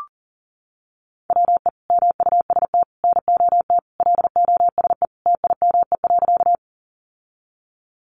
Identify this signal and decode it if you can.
Morse code, heard as a keyed tone, about 37 wpm, 700 Hz. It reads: WE MUST NOT LOSE TIME.
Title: Sentences from Top 1100 Words 40wpm